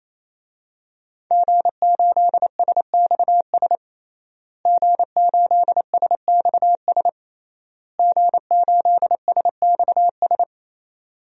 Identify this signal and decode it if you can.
Morse code, heard as a keyed tone, about 27 wpm, 700 Hz. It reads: G8SXS G8SXS G8SXS
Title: DXCC-100 Call Signs - 5 Characters - Repeated 3x - 28wpm